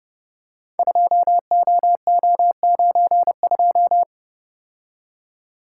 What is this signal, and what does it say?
Morse code, heard as a keyed tone, about 30 wpm, 700 Hz. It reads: 2OO92
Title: Code Groups - 5 Letter:Numbers - 30wpm